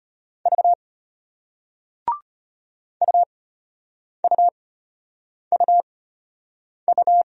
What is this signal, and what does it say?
Morse code, heard as a keyed tone, about 32 wpm, spaced as for 8 wpm, 700 Hz. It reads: V UUUU